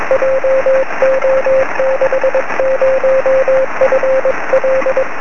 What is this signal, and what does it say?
Morse code, heard as a keyed tone, about 21 wpm, 530 Hz, with noise. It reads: JO60FL
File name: sm7lcb / 20 23cm / 2129 dl3ww 23cm